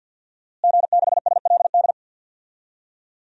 Morse code, heard as a keyed tone, about 50 wpm, 700 Hz. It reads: G6SLD
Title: DXCC-100 Call Signs - 5 Characters - Repeated 3x - 50wpm